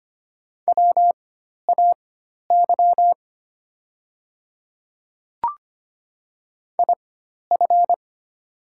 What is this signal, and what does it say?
Morse code, heard as a keyed tone, about 25 wpm, spaced as for 12 wpm, 700 Hz. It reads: WAY IF
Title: 4x Character Spacing - Top 100 Words - 25wpm